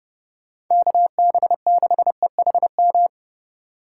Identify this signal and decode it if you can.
Morse code, heard as a keyed tone, about 30 wpm, 700 Hz. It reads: KB6EHM